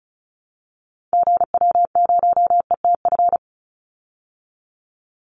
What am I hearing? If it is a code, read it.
Morse code, heard as a keyed tone, about 34 wpm, 700 Hz. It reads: GW0ETF
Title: Call Signs Repeated 2x - 35wpm